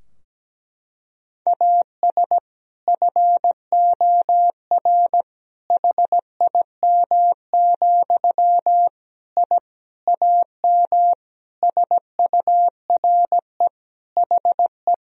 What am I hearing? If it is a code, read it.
Morse code, heard as a keyed tone, about 17 wpm, 700 Hz. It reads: AS FOR HIM, I AM SURE HE
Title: Sentences from Top 1000 Words - Review Part 2 - 17wpm